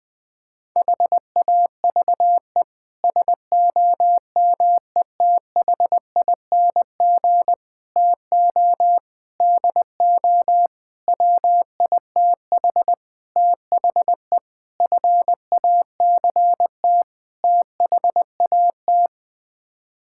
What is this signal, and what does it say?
Morse code, heard as a keyed tone, about 20 wpm, 700 Hz. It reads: HAVE SOMETHING TO DO WITH THE FACT THAT
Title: Sets of 8 Words - 20wpm